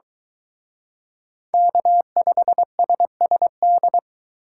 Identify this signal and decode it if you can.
Morse code, heard as a keyed tone, about 23 wpm, 700 Hz. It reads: K5SSD